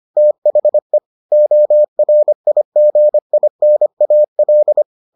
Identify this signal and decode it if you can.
Morse code, heard as a keyed tone, about 25 wpm, 600 Hz. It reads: THE ORIGINAL